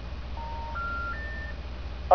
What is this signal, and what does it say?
Special information tone, with noise.